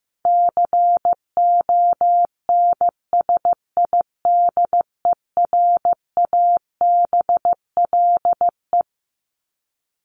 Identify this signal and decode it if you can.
Morse code, heard as a keyed tone, about 15 wpm, 700 Hz. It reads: CONSIDERABLE
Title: LY Suffix - Top 100 Words with Base Words 15wpm